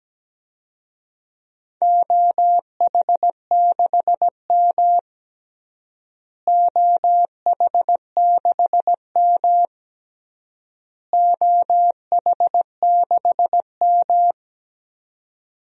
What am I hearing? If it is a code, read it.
Morse code, heard as a keyed tone, about 17 wpm, 700 Hz. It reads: OH6M OH6M OH6M
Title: DXCC-100 Call Signs - 4 Characters - Repeated 3x - 17wpm